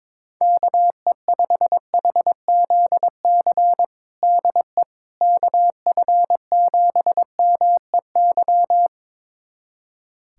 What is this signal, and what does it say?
Morse code, heard as a keyed tone, about 22 wpm, 700 Hz. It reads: KE5HZC DE KF7MEY
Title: USA Callsign DE Callsign - 6 Characters - 22wpm